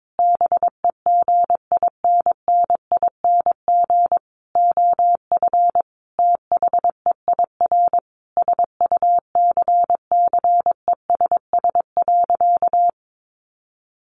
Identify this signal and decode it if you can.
Morse code, heard as a keyed tone, about 22 wpm, 700 Hz. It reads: BEGINNING OF THEIR SUCCESS.
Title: Sentences from Top 900 Words 22wpm